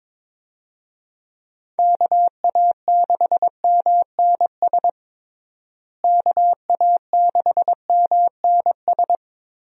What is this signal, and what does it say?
Morse code, heard as a keyed tone, about 22 wpm, 700 Hz. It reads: KA6MNS KA6MNS
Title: USA Call Signs - 6 Characters - Repeated 3x - 22wpm